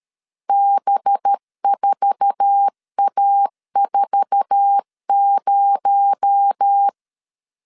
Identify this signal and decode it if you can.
Morse code, heard as a keyed tone, about 13 wpm, 800 Hz. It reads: B4A40
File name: LEZIONI / 8 YJ45 / 8 YJ4 12wpm 2151